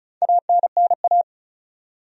Morse code, heard as a keyed tone, about 36 wpm, 700 Hz. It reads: ANNA
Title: USA Common Names - Max 5 Letters - 35wpm